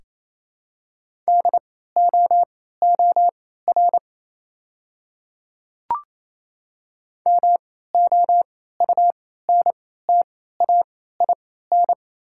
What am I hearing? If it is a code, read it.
Morse code, heard as a keyed tone, about 28 wpm, spaced as for 16 wpm, 690 Hz. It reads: DOOR MOUNTAIN
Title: 3x Character Spacing - Top 300 Words - 28wpm